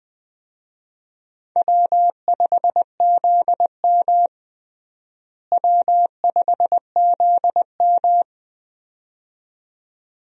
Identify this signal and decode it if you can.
Morse code, heard as a keyed tone, about 20 wpm, 700 Hz. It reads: W5ZM W5ZM
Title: USA Call Signs - 4 Characters - Repeated 2x - 20wpm